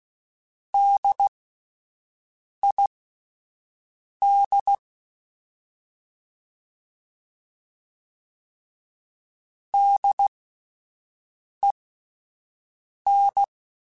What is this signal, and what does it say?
Morse code, heard as a keyed tone, about 16 wpm, spaced as for 5 wpm, 800 Hz. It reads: DID DEN
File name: Spaced Alphas / Spaced Alphas mp3s / Ds sp @10